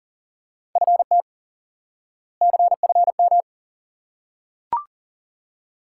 Morse code, heard as a keyed tone, about 40 wpm, 700 Hz. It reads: FT CFM
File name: course / instant-qso-element-course-3 / IQE Course 3 - Lesson 89 - YAGI - Two Elements - 40wpm